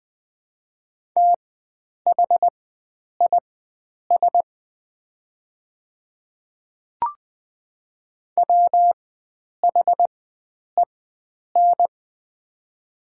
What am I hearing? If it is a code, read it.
Morse code, heard as a keyed tone, about 21 wpm, spaced as for 9 wpm, 700 Hz. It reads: THIS WHEN
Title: 4x Character Spacing - Top 100 Words - 20wpm